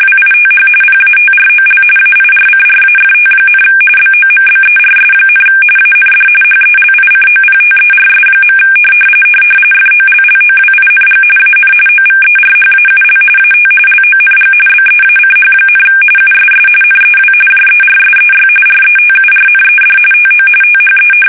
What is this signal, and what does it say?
RTTY, about 45.45 baud. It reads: WITH PATROL AIRCRAFT, BOMBERS, AND HELICOPTERS. FOLLOWING AN OVERHAUL AT LONG BEACH, USS NEW JERSEY RETURNED TO THE
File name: fsk850 / battleshipnj